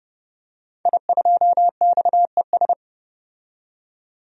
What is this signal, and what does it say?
Morse code, heard as a keyed tone, about 30 wpm, 700 Hz. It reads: I2XES
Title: DXCC-100 Call Signs - 5 Characters - 30wpm